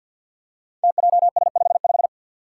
Morse code, heard as a keyed tone, about 51 wpm, 700 Hz. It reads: TJS55
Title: Five Challenging Letter-Number Combinations 50wpm